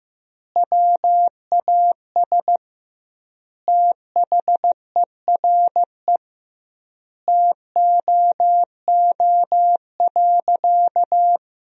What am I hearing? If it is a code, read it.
Morse code, heard as a keyed tone, about 15 wpm, 700 Hz. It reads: WAS THERE TOO.
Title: 2x Word-Spacing - Sentences from Top 200 Words - 15wpm